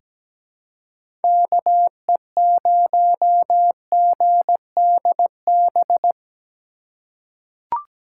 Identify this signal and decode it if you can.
Morse code, heard as a keyed tone, about 17 wpm, 700 Hz. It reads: KE0GDB
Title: USA Call Signs - 6 Characters - 4x Character Spacing - 17wpm